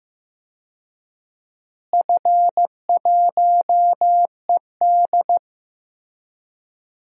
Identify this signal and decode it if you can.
Morse code, heard as a keyed tone, about 15 wpm, 700 Hz. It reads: F1ED
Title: DXCC-100 Call Signs - 4 Characters - Repeated 3x - 15wpm